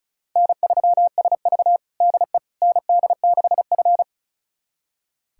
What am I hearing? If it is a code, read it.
Morse code, heard as a keyed tone, about 36 wpm, 700 Hz. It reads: N3SV DE ND6F
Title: USA Callsign DE Callsign - 4 Characters - 35wpm